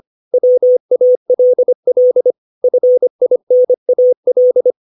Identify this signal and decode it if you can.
Morse code, heard as a keyed tone, about 25 wpm, 500 Hz. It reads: WALL FINAL
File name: 632 words / 632 Words 25WPM